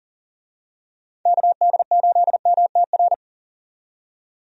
Morse code, heard as a keyed tone, about 39 wpm, 700 Hz. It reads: KD8MTR
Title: USA Call Signs - 6 Characters - 40wpm